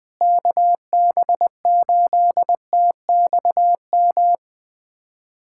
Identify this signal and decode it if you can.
Morse code, heard as a keyed tone, about 20 wpm, 700 Hz. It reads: KB8TXM
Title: USA Callsign DE Callsign 20wpm